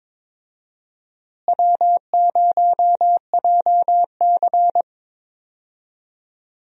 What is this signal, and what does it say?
Morse code, heard as a keyed tone, about 22 wpm, 700 Hz. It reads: W0JC
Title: USA Call Signs - 4 Characters - Repeated 3x - 22wpm